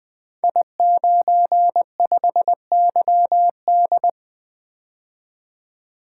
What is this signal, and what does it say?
Morse code, heard as a keyed tone, about 20 wpm, 700 Hz. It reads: I95YD
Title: Code Groups - 5 Letter:Numbers - 20wpm